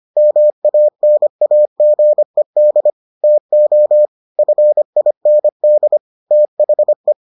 Morse code, heard as a keyed tone, about 25 wpm, 600 Hz. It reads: MANAGED TO FIND THE